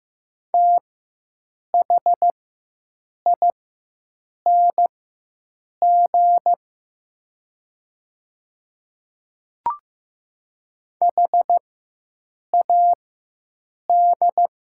Morse code, heard as a keyed tone, about 15 wpm, spaced as for 7 wpm, 700 Hz. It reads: THING HAD
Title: 4x Character Spacing - Top 100 Words - 15wpm